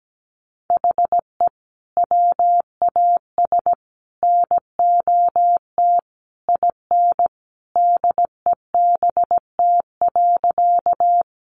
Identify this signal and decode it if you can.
Morse code, heard as a keyed tone, about 17 wpm, 700 Hz. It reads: HE WAS NOT IN DEBT.
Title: Sentences from Top 1300 Words 17wpm